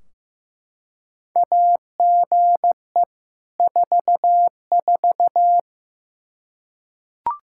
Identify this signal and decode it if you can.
Morse code, heard as a keyed tone, about 15 wpm, 700 Hz. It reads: AGE 44
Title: Practice QSO - Exchange 3 - More Info - 15wpm